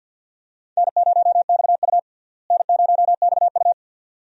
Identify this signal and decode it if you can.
Morse code, heard as a keyed tone, about 53 wpm, 700 Hz. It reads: N0XU N0XU
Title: USA Call Signs - 4 Characters - Repeated 2x - 50wpm